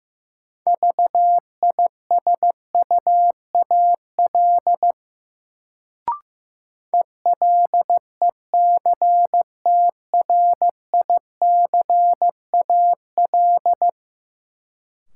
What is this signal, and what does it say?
Morse code, heard as a keyed tone, about 15 wpm, 700 Hz. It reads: VISUAL ELECTRICAL
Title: Sentences from Top 1000 Words - Review Part 6 - New Word Drill - 15wpm